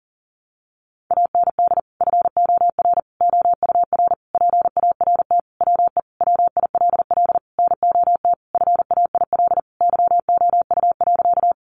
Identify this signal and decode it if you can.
Morse code, heard as a keyed tone, about 40 wpm, 700 Hz. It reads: AND FOR OUR PART WE WILL NOT FAIL YOU.
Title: Sentences from Top 1100 Words 40wpm